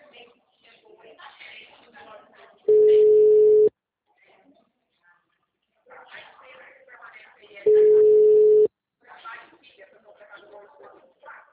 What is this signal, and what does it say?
Ringback tone (Europe, 425 Hz), with speech.